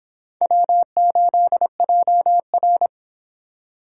Morse code, heard as a keyed tone, about 31 wpm, 700 Hz. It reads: W8JR